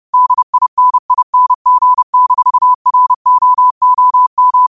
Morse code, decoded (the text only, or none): DINING-ROOM